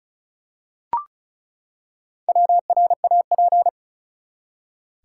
Morse code, heard as a keyed tone, about 34 wpm, 700 Hz. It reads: WRAP
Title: Four Letter Words - 35wpm